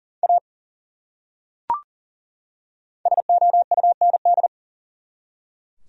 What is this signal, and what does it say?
Morse code, heard as a keyed tone, about 39 wpm, 700 Hz. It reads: A SOUND
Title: Top 100 Words 40wpm